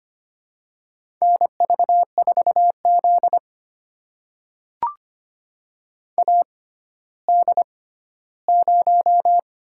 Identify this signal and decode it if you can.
Morse code, heard as a keyed tone, about 25 wpm, spaced as for 14 wpm, 700 Hz. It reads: NV4Z AD0
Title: USA Call Signs - 4 Characters - 6x Character Spacing - 25wpm